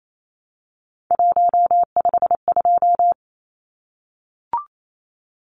Morse code, heard as a keyed tone, about 28 wpm, 700 Hz. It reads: 152
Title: Numbers - 3-Digit 28wpm